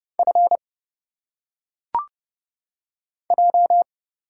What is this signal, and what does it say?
Morse code, heard as a keyed tone, about 29 wpm, 700 Hz. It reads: F J
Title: Single Letters - Phonetic Pronunciation 30wpm